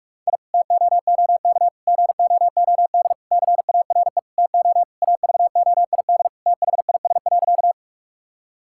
Morse code, heard as a keyed tone, about 46 wpm, 690 Hz. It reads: I TOOK GOOD CARE TO AVOID THIS.
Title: Sentences from Top 1000 Words - Review Part 6 - 45wpm